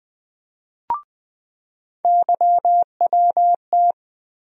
Morse code, heard as a keyed tone, about 20 wpm, 700 Hz. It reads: YWT
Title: USA Call Sign Postfixes - 3 Letters 20wpm